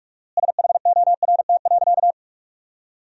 Morse code, heard as a keyed tone, about 42 wpm, 700 Hz. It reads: SHORT.
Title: 4x Word-Spacing - Sentences from Top 500 Words - 45wpm